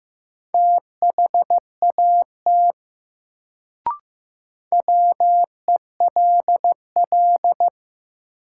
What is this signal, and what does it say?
Morse code, heard as a keyed tone, about 15 wpm, 700 Hz. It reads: THAT WELL
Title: Sets of 2 Words - New Word Drill - 15wpm